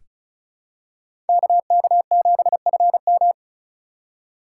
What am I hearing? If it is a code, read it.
Morse code, heard as a keyed tone, about 36 wpm, 700 Hz. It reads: KK7FM
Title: USA Call Signs - 4x Character Spacing - 35wpm